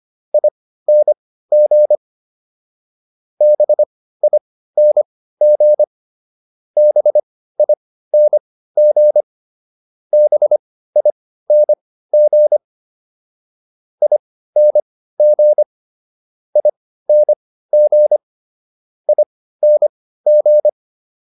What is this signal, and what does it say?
Morse code, heard as a keyed tone, about 25 wpm, spaced as for 15 wpm, 600 Hz. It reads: ING BING BING BING ING ING ING